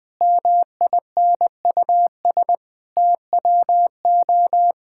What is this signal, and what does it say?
Morse code, heard as a keyed tone, about 20 wpm, 700 Hz. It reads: MINUS TWO
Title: Sentences with 8-Word Phrases - Spoken Beginning - 20wpm